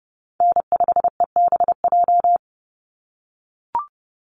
Morse code, heard as a keyed tone, about 30 wpm, 700 Hz. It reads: N5EBJ
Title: USA Call Signs 30wpm